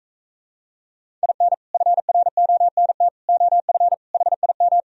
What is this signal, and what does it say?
Morse code, heard as a keyed tone, about 43 wpm, 700 Hz. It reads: IN FRONT OF HIM